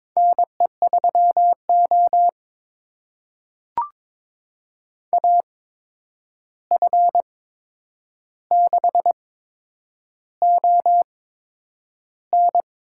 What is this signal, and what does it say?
Morse code, heard as a keyed tone, about 22 wpm, spaced as for 9 wpm, 700 Hz. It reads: NE3O AF6ON
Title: USA Call Signs - 8x Character Spacing - 22wpm